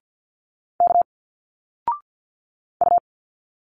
Morse code, heard as a keyed tone, about 50 wpm, 700 Hz. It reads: K U
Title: Single Letters 50wpm